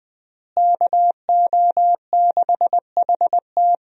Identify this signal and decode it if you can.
Morse code, heard as a keyed tone, about 20 wpm, 700 Hz. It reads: KO6HT
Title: USA Call Signs - 5 Characters - Repeated 2x - 20wpm